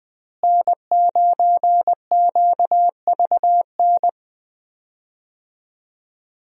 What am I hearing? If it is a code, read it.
Morse code, heard as a keyed tone, about 20 wpm, 700 Hz. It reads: N9QVN
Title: USA Call Signs - 5 Characters - Repeated 2x - 20wpm